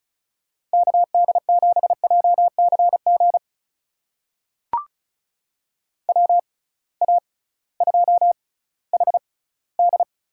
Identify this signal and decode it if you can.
Morse code, heard as a keyed tone, about 36 wpm, spaced as for 19 wpm, 700 Hz. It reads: KD7JCG WA2HD
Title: USA Call Signs - 6 Characters - 6x Character Spacing - 35wpm